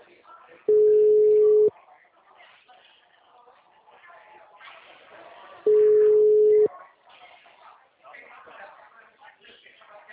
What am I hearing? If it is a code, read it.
Ringback tone (Europe, 425 Hz), with speech.